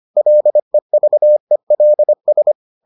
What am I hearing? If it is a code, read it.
Morse code, heard as a keyed tone, about 25 wpm, 600 Hz. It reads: LEVELS